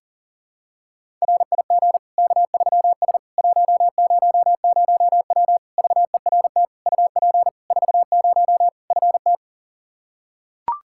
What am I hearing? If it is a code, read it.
Morse code, heard as a keyed tone, about 40 wpm, 700 Hz. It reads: RIG K3S 100W VERT UP 40 FT
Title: K1USN SST QSO Practice 40wpm